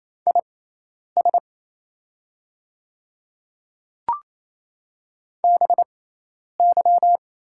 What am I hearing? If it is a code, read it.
Morse code, heard as a keyed tone, about 28 wpm, spaced as for 10 wpm, 700 Hz. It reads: IS BY
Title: 6x Character Spacing - Top 100 Words - 28wpm